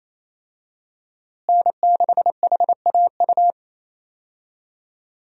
Morse code, heard as a keyed tone, about 28 wpm, 700 Hz. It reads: N6HAU
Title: USA Call Signs - 5 Characters - Repeated 2x - 28wpm